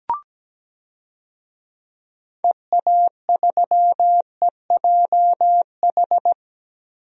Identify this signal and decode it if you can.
Morse code, heard as a keyed tone, about 17 wpm, 700 Hz. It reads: EA3EJH
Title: DXCC-100 Call Signs - 6 Characters - 17wpm